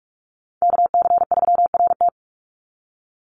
Morse code, heard as a keyed tone, about 44 wpm, 700 Hz. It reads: KC3RT
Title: NA SOTA Chaser Callsigns 45wpm